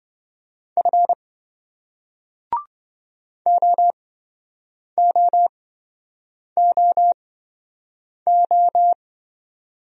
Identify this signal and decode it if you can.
Morse code, heard as a keyed tone, about 27 wpm, spaced as for 7 wpm, 700 Hz. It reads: F OOOO